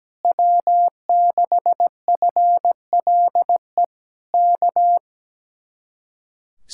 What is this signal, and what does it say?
Morse code, heard as a keyed tone, about 17 wpm, 700 Hz. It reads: W6FLE K
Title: Speed-Racing Common QSO Phrases - 17wpm